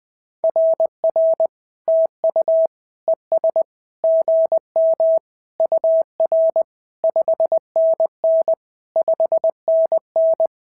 Morse code, heard as a keyed tone, about 20 wpm, 690 Hz. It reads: RR TU ES GM UR 5NN 5NN
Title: SOTA QSO Practice 20wpm